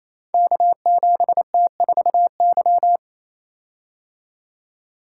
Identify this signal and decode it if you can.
Morse code, heard as a keyed tone, about 27 wpm, 700 Hz. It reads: K7T4Y
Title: Code Groups - 5 Letter:Numbers - 28wpm